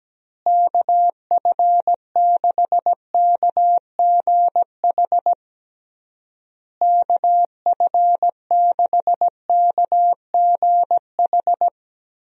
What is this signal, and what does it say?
Morse code, heard as a keyed tone, about 17 wpm, 700 Hz. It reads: KF6KGH KF6KGH
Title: USA Call Signs - 6 Characters - Repeated 3x - 17wpm